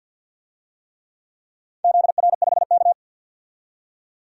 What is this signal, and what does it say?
Morse code, heard as a keyed tone, about 47 wpm, 700 Hz. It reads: ZR5K
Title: DXCC-100 Call Signs - 4 Characters - 50wpm